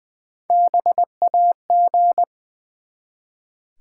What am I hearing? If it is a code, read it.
Morse code, heard as a keyed tone, about 20 wpm, 700 Hz. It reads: BAG
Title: Easy Words with Hints - 20wpm